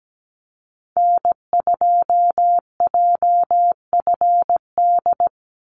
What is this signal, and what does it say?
Morse code, heard as a keyed tone, about 17 wpm, 700 Hz. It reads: N2JFD
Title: Call Signs Repeated 2x - 17wpm